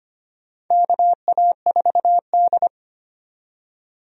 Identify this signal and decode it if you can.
Morse code, heard as a keyed tone, about 25 wpm, 700 Hz. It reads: KA4D
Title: USA Call Signs - 4 Characters - 25wpm